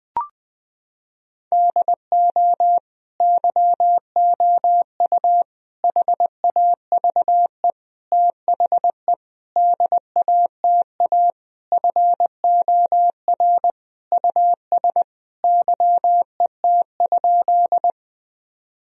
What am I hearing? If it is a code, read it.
Morse code, heard as a keyed tone, about 20 wpm, 700 Hz. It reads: DO YOU HAVE THE DATA FOR US YET?
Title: Sentences from Top 1000 Words - Review Part 3 - 20wpm